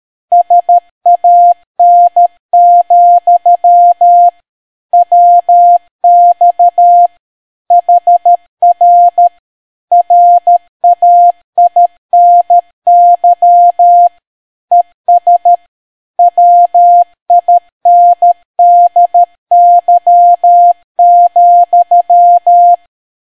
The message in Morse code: SAN, WX HR RAINY ES WINDY,